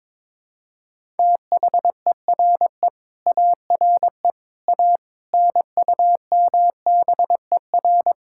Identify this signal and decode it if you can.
Morse code, heard as a keyed tone, about 23 wpm, 700 Hz. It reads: THERE ARE A NUMBER
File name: practice / 22wpm / Build A Phrase - 5-Word Phrases - 22wpm